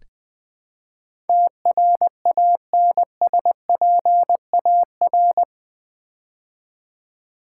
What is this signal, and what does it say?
Morse code, heard as a keyed tone, about 20 wpm, 700 Hz. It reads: TRANSPAR
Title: Build A Word - 4 Letters At A Time - 12-Letter Words - 20wpm